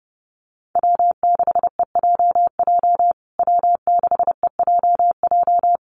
Morse code, heard as a keyed tone, about 30 wpm, 700 Hz. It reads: W6EJJ W6EJJ
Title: USA Call Signs Repeated 3x 30wpm